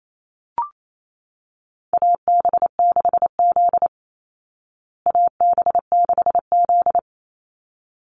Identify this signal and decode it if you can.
Morse code, heard as a keyed tone, about 28 wpm, 700 Hz. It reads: AB6Z AB6Z
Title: Call Signs Repeated 3x - 28wpm